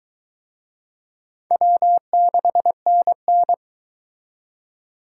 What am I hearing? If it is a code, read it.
Morse code, heard as a keyed tone, about 22 wpm, 700 Hz. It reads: W6NN